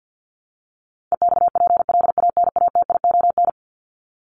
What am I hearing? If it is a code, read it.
Morse code, heard as a keyed tone, about 50 wpm, 700 Hz. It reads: EXPLANATION
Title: Top 1751-2000 Words 50wpm